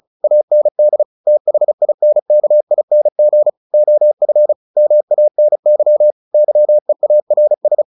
Morse code, heard as a keyed tone, about 34 wpm, 600 Hz. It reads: AND THINKING OF MANY YEARS
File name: the old man and the sea / the old man and the sea 1 35wpm